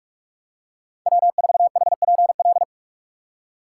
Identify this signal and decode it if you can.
Morse code, heard as a keyed tone, about 44 wpm, 700 Hz. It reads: W4HPL